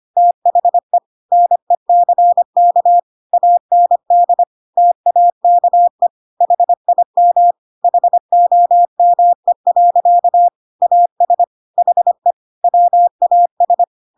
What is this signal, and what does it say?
Morse code, heard as a keyed tone, about 25 wpm, 700 Hz. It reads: THE NECK AND TAKE HIM HOME. AS HE WAS
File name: Pinocchio / Pinocchio ch3 25 25 ik1hge